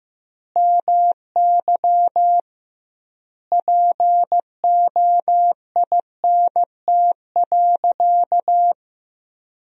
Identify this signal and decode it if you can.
Morse code, heard as a keyed tone, about 15 wpm, 700 Hz. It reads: MY POINT.
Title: 2x Word-Spacing - Sentences from Top 200 Words - 15wpm